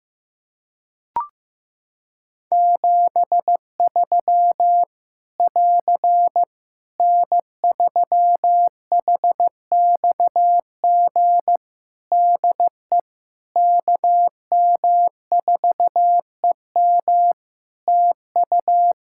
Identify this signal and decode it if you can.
Morse code, heard as a keyed tone, about 15 wpm, 700 Hz. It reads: 73 <AR> N3HXG DE KM4EM TU
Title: Practice QSO - Exchange 4 - Ending - 15wpm